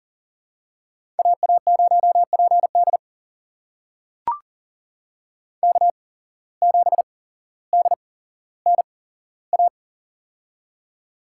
Morse code, heard as a keyed tone, about 41 wpm, spaced as for 17 wpm, 700 Hz. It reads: AA0PD K7DNA
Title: USA Call Signs - 5 Characters - 8x Character Spacing - 40wpm